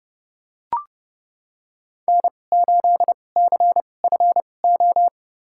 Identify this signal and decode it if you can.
Morse code, heard as a keyed tone, about 30 wpm, spaced as for 22 wpm, 700 Hz. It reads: N8CFO
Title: USA Call Signs - 5 Characters - 2x Character Spacing - 30wpm